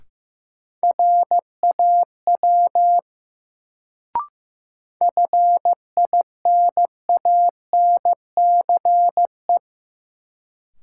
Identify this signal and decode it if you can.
Morse code, heard as a keyed tone, about 15 wpm, 700 Hz. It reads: RAW FINANCE
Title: Sentences from Top 1500 Words - New Word Drill - 15wpm